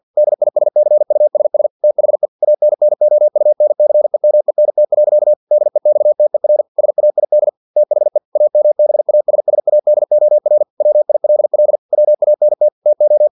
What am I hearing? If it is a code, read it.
Morse code, heard as a keyed tone, about 50 wpm, 600 Hz. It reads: DISCUSS THE ANNOUNCEMENT. DEXTER SAID THE AMBASSADOR WILL WANT TO
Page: Never 49wpm